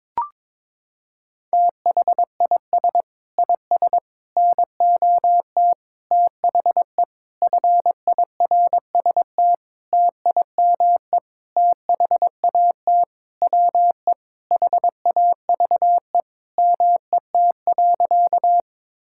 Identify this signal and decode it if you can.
Morse code, heard as a keyed tone, about 22 wpm, 700 Hz. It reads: THIS IS NOT THE FIRST TIME THAT WE HAVE MET.
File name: Sentences with 7-Word Phrases - 22wpm